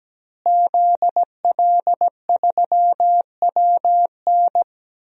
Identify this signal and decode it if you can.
Morse code, heard as a keyed tone, about 17 wpm, 700 Hz. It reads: ZL3WN
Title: DXCC-100 Call Signs - 5 Characters - Repeated 3x - 17wpm